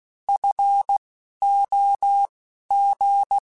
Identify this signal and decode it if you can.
Morse code, heard as a keyed tone, about 16 wpm, spaced as for 11 wpm, 800 Hz. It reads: FOG
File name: Alphabetical / Alphabetical mp3s / Fs @10